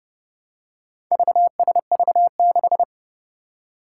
Morse code, heard as a keyed tone, about 30 wpm, 700 Hz. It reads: VSV6
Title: Four Challenging Letter-Number Combinations 30wpm